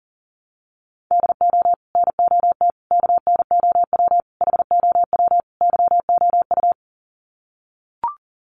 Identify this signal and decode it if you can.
Morse code, heard as a keyed tone, about 37 wpm, 700 Hz. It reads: DO NOT KNOW HOW YOU
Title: Sets of 5 Words Encore 40wpm